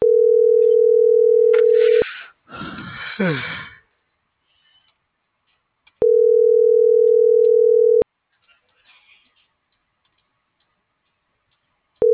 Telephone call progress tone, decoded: ringback tone

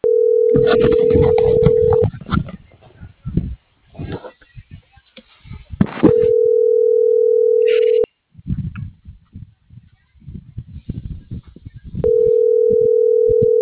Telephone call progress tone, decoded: ringback tone

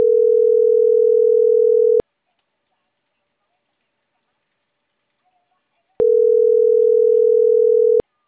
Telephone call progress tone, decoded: ringback tone

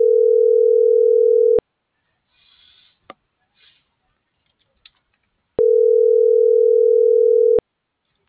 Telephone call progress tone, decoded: ringback tone

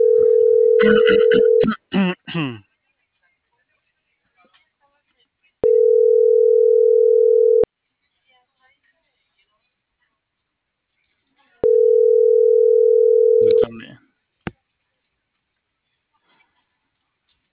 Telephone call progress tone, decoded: ringback tone

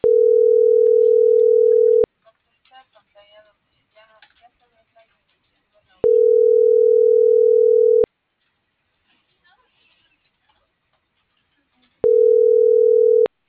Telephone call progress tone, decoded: ringback tone

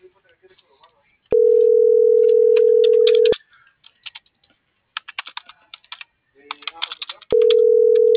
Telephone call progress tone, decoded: ringback tone

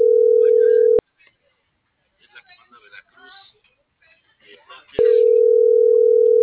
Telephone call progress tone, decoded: ringback tone

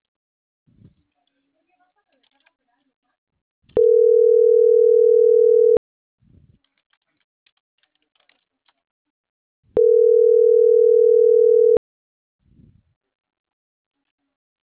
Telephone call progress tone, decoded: ringback tone

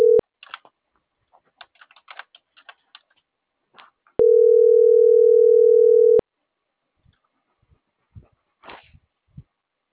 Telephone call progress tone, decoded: ringback tone

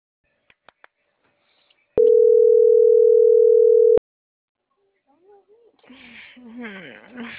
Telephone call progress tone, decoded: ringback tone

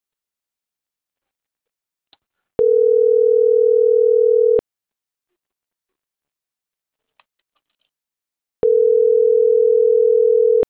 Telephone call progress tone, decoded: ringback tone

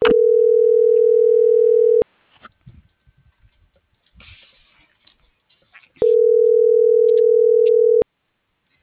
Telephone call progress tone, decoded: ringback tone

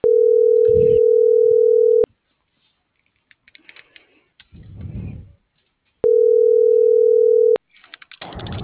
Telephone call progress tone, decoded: ringback tone

